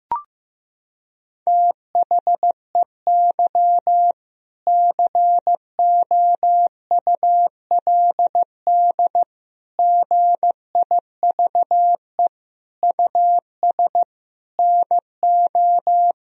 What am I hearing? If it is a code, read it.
Morse code, heard as a keyed tone, about 15 wpm, 700 Hz. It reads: THEY COULD GIVE US NO
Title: Sentences from Top 1000 Words - Review Part 3 - 15wpm